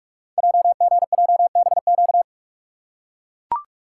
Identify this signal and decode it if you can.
Morse code, heard as a keyed tone, about 44 wpm, 690 Hz. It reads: JGJBQ
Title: Code Groups - 5 Letters - 45wpm